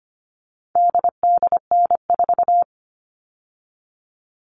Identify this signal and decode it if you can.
Morse code, heard as a keyed tone, about 25 wpm, 700 Hz. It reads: DDN4
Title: All Cut Numbers - 4-Digit 25wpm